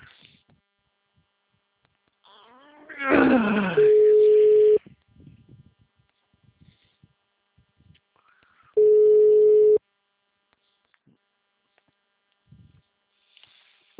Ringback tone (Europe, 425 Hz).
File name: out- -2007- - - .1124